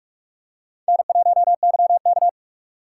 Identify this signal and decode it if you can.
Morse code, heard as a keyed tone, about 44 wpm, 700 Hz. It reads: N1YK
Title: USA Call Signs - 2x Character Spacing - 45wpm